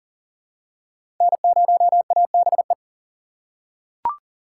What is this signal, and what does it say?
Morse code, heard as a keyed tone, about 40 wpm, 700 Hz. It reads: N0ABE
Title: USA Call Signs - 5 Characters - 2x Character Spacing - 40wpm